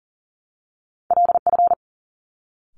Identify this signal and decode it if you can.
Morse code, heard as a keyed tone, about 39 wpm, 700 Hz. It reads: LF
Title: Top Two Letter Combinations in Words 40wpm